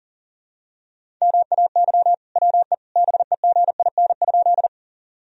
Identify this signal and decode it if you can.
Morse code, heard as a keyed tone, about 40 wpm, 700 Hz. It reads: MAY WE BEGIN?
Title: Sentences from Top 1000 Words - Review Part 1 - 40wpm